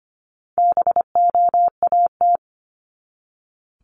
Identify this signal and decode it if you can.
Morse code, heard as a keyed tone, about 25 wpm, 700 Hz. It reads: BOAT
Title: Top 500 Words 25wpm